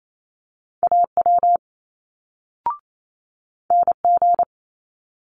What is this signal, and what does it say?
Morse code, heard as a keyed tone, about 28 wpm, 700 Hz. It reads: AW NG
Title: Multi-letter Phonemes Quick Fire 28wpm